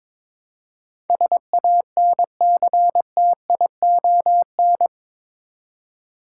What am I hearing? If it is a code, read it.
Morse code, heard as a keyed tone, about 22 wpm, 700 Hz. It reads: SANCTION
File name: practice / 22wpm / Eight Letter Words - 22wpm